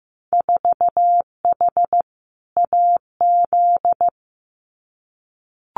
Morse code, heard as a keyed tone, about 15 wpm, 700 Hz. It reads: 4H AZ
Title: Winter Field Day Exchanges Encore 15wpm